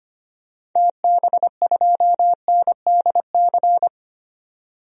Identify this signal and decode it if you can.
Morse code, heard as a keyed tone, about 25 wpm, 700 Hz. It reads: TB2NDC
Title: DXCC-100 Call Signs - 6 Characters - 25wpm